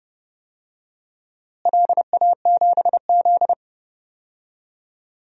Morse code, heard as a keyed tone, about 30 wpm, 700 Hz. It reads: LA7Z
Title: DXCC-100 Call Signs - 4 Characters - Repeated 3x - 30wpm